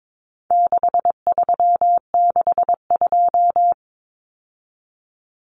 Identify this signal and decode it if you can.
Morse code, heard as a keyed tone, about 22 wpm, 700 Hz. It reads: 6362
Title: Numbers - 4-Digit 22wpm